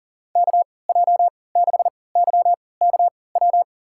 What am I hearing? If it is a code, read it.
Morse code, heard as a keyed tone, about 37 wpm, spaced as for 22 wpm, 690 Hz. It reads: KJ6YKW
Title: USA Call Signs - 3x Character Spacing - 40wpm